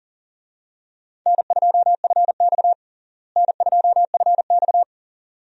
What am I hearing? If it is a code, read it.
Morse code, heard as a keyed tone, about 40 wpm, 700 Hz. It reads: N2FX N2FX
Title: USA Call Signs - 4 Characters - Repeated 2x - 40wpm